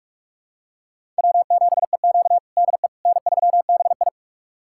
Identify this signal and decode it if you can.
Morse code, heard as a keyed tone, about 41 wpm, 700 Hz. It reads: W7EQ DE N3BI
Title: USA Callsign DE Callsign - 4 Characters - 45wpm